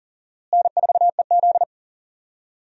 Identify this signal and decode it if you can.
Morse code, heard as a keyed tone, about 40 wpm, 700 Hz. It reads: N4EZ
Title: USA Call Signs - 4 Characters - Repeated 3x - 40wpm